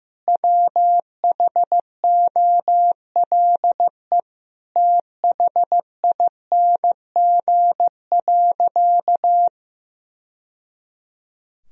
Morse code, heard as a keyed tone, about 15 wpm, 700 Hz. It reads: WHOLE THING.
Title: Sentences from Top 1000 Words - Review Part 1 - 15wpm